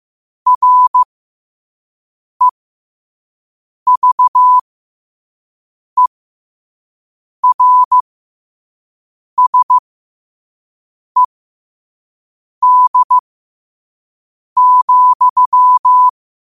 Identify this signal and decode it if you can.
Morse code, heard as a keyed tone, about 15 wpm, spaced as for 5 wpm, 1000 Hz. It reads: REVERSED,
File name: qotdfiles / 5wpm